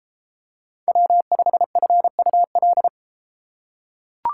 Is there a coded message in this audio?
Yes: Morse code from 0.9 to 2.9 s, heard as a keyed tone, about 31 wpm, 700 Hz. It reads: W5FUL